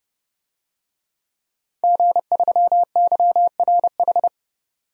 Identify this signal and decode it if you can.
Morse code, heard as a keyed tone, about 32 wpm, 700 Hz. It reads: G3YRH
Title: DXCC-100 Call Signs - 5 Characters - 30wpm